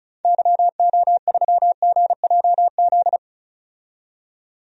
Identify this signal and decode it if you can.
Morse code, heard as a keyed tone, about 36 wpm, 700 Hz. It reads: YO3GJZ
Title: DXCC-100 Call Signs - 6 Characters - 35wpm